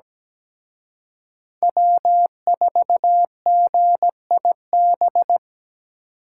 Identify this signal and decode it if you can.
Morse code, heard as a keyed tone, about 20 wpm, 700 Hz. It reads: W4GIB
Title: Speed Racing - USA 1x3 Call Signs 17wpm